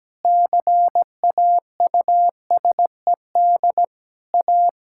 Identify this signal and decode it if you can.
Morse code, heard as a keyed tone, about 17 wpm, 700 Hz. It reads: CAUSED A
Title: Pangram Challenge v2 - 17wpm